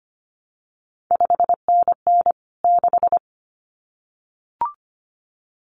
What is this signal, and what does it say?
Morse code, heard as a keyed tone, about 25 wpm, 700 Hz. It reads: 5NN 6
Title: CQ WW DX Contest Exchange Encore - 25wpm